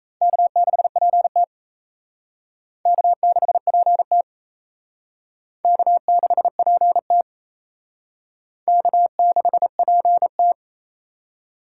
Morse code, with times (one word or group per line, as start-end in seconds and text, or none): 0.2-1.4 K6PT
2.9-4.2 K6PT
5.6-7.2 K6PT
8.7-10.5 K6PT